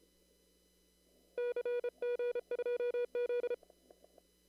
Morse code, heard as a keyed tone, about 31 wpm, 500 Hz. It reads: CG2Z
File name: CW / kt5j wpx cw-op-k5pi